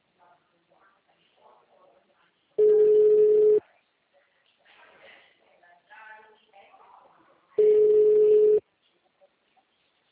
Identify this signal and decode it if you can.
Ringback tone (Europe, 425 Hz), with speech.